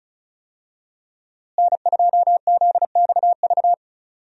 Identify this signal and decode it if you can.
Morse code, heard as a keyed tone, about 36 wpm, 700 Hz. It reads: N2ZXV
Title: USA Call Signs - 5 Characters - Repeated 3x - 35wpm